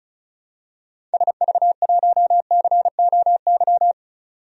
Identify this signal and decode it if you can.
Morse code, heard as a keyed tone, about 36 wpm, 700 Hz. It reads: SV1COY
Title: DXCC-100 Call Signs - 6 Characters - 35wpm